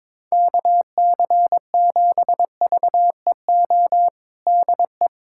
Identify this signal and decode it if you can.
Morse code, heard as a keyed tone, about 22 wpm, 700 Hz. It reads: KC7VEO DE
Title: USA Callsign DE Callsign - 6 Characters - 22wpm